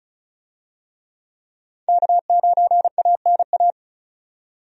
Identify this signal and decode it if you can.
Morse code, heard as a keyed tone, about 36 wpm, 700 Hz. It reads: K9ANA
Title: USA Call Signs - 5 Characters - 35wpm